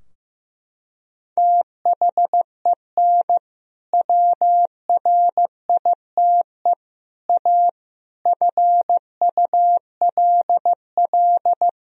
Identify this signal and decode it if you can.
Morse code, heard as a keyed tone, about 15 wpm, 700 Hz. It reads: THEN WRITE A FULL
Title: Sentences from Top 1000 Words - Review Part 3 - 15wpm